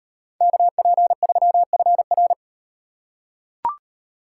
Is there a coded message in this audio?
Yes: Morse code from 0.4 to 2.3 s, heard as a keyed tone, about 38 wpm, 700 Hz. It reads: KP3FR